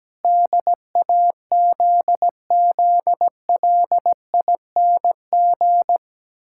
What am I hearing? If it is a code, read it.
Morse code, heard as a keyed tone, about 17 wpm, 700 Hz. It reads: DAZZLING
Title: Pangram Challenge v2 - 17wpm